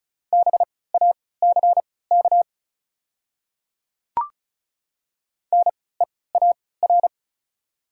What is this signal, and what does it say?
Morse code, heard as a keyed tone, about 36 wpm, spaced as for 20 wpm, 700 Hz. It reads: BACK NEAR
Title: 3x Character Spacing - Top 200 Words - 35wpm